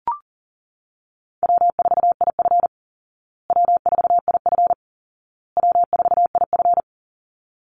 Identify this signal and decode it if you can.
Morse code, heard as a keyed tone, about 40 wpm, 700 Hz. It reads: W4IF W4IF W4IF
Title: Easy Call Signs Repeated 3x - 40wpm